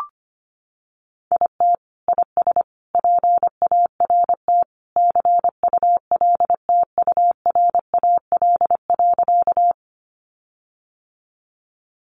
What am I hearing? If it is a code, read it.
Morse code, heard as a keyed tone, about 25 wpm, 700 Hz. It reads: IT IS PART CULTURAL.
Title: Sentences from Top 1500 Words 25wpm